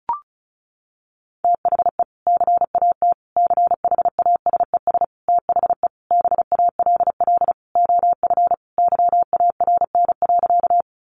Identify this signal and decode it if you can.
Morse code, heard as a keyed tone, about 34 wpm, 700 Hz. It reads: THE CAT CHASES THE BALL OF YARN.
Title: Additional Easy Phrases 35wpm